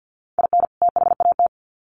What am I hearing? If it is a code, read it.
Morse code, heard as a keyed tone, about 50 wpm, 700 Hz. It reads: IN THAT
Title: Sets of 2 Words 50wpm